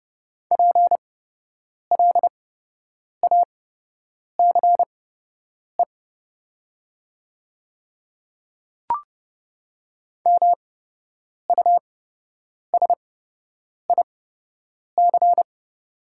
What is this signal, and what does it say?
Morse code, heard as a keyed tone, about 30 wpm, spaced as for 8 wpm, 700 Hz. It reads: PLACE MUSIC
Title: 8x Character Spacing - Top 300 Words - 30wpm